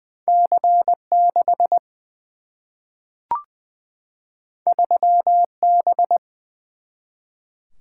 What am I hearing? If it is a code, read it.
Morse code, heard as a keyed tone, about 20 wpm, 700 Hz. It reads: C6 3B
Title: Single-Letter Single-Number 20wpm